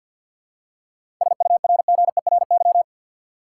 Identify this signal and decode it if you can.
Morse code, heard as a keyed tone, about 45 wpm, 700 Hz. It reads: SURGERY
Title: Seven Letter Words - 50wpm